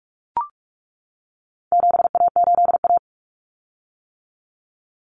Morse code, heard as a keyed tone, about 44 wpm, 700 Hz. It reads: 7A8A
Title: Common Cut Numbers - 4-Digit 45wpm